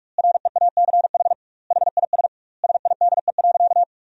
Morse code, heard as a keyed tone, about 42 wpm, 700 Hz. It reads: REACH HIS SIDE.
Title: 2x Word-Spacing - Sentences from Top 400 Words - 45wpm